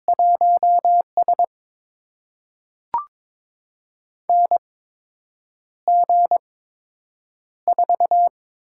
Morse code, heard as a keyed tone, about 21 wpm, spaced as for 8 wpm, 700 Hz. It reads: 1S NG4
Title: USA Call Signs - 4 Characters - 8x Character Spacing - 22wpm